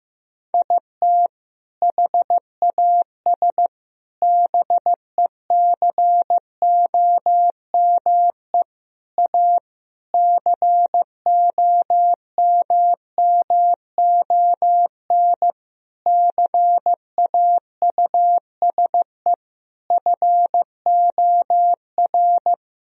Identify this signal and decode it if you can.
Morse code, heard as a keyed tone, about 15 wpm, 700 Hz. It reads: IT HAS BECOME A COMMON CAUSE FOR
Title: Sentences from Top 1000 Words - Review Part 3 - 15wpm